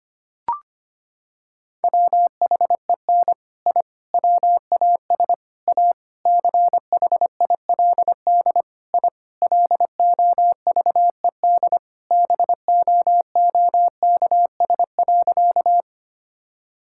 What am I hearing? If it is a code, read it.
Morse code, heard as a keyed tone, about 25 wpm, 700 Hz. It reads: WHEN I WAS A CHILD I LOVED BOOKS.
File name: practice / 25wpm / Sentences with 3-Word Phrases - 25wpm